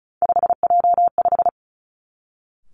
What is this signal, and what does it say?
Morse code, heard as a keyed tone, about 36 wpm, 700 Hz. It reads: 5J5
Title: Three Challenging Letter-Number Combinations 35wpm